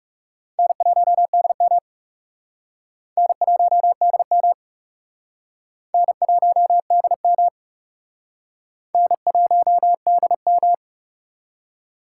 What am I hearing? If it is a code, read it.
Morse code, heard as a keyed tone, about 38 wpm, 700 Hz. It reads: N1DM N1DM N1DM N1DM
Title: Speed Racing - USA 1x2 Call Signs 30wpm